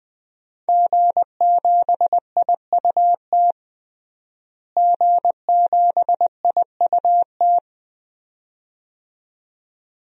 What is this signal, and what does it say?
Morse code, heard as a keyed tone, about 20 wpm, 700 Hz. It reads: G7IUT G7IUT
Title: DXCC-100 Call Signs - 5 Characters - Repeated 2x - 20wpm